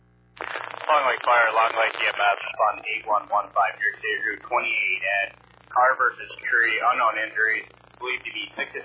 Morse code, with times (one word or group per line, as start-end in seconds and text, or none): none